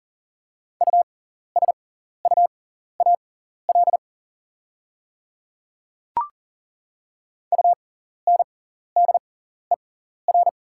Morse code, heard as a keyed tone, about 40 wpm, spaced as for 14 wpm, 690 Hz. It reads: USUAL UNDER
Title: 6x Character Spacing - Top 500 Words - 40wpm